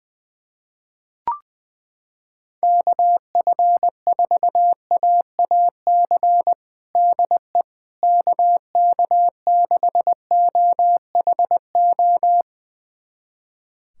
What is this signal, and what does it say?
Morse code, heard as a keyed tone, about 20 wpm, 700 Hz. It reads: KF4AAC DE KK6OHO
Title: USA Callsign DE Callsign 20wpm